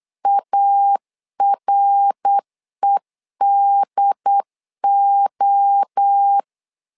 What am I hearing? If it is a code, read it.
Morse code, heard as a keyed tone, about 8 wpm, 800 Hz. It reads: AREDO
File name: 4 Q 8wpm